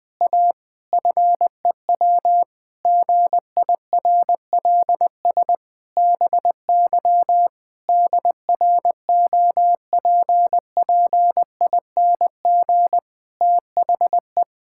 Morse code, heard as a keyed tone, about 21 wpm, 700 Hz. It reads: A FEW GIRLS BY DROPPING THE
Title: Pangram Challenge v2 - 20wpm